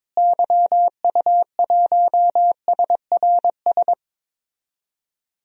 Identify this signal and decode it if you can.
Morse code, heard as a keyed tone, about 22 wpm, 700 Hz. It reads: YU1SRS
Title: DXCC-100 Call Signs - 6 Characters - Repeated 2x - 22wpm